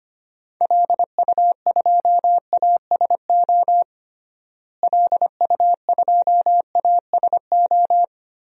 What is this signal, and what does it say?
Morse code, heard as a keyed tone, about 25 wpm, 700 Hz. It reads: LU2ASO LU2ASO
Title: DXCC-100 Call Signs - 6 Characters - Repeated 3x - 25wpm